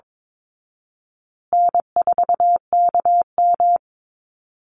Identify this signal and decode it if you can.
Morse code, heard as a keyed tone, about 22 wpm, 700 Hz. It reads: N4KM
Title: Easy Call Signs Repeated 2x - 22wpm